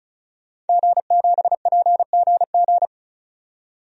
Morse code, heard as a keyed tone, about 34 wpm, 700 Hz. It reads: G7PGG
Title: DXCC-100 Call Signs - 5 Characters - 35wpm